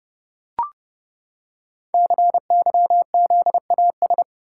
Morse code, heard as a keyed tone, about 29 wpm, 690 Hz. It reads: CYZAS
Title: Code Groups - 5 Letters - 30wpm